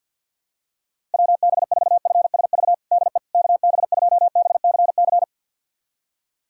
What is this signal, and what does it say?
Morse code, heard as a keyed tone, about 46 wpm, 700 Hz. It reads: WB4FSV DE KB2BCC
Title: USA Callsign DE Callsign - 6 Characters - 50wpm